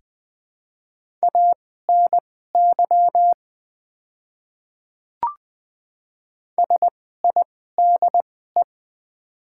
Morse code, heard as a keyed tone, about 20 wpm, spaced as for 14 wpm, 700 Hz. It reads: ANY SIDE
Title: 2x Character Spacing - Top 200 Words - 20wpm